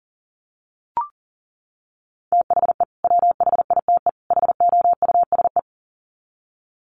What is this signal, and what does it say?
Morse code, heard as a keyed tone, about 40 wpm, 700 Hz. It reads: THE WHITE HOUSE
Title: Sets of 3 Words 40wpm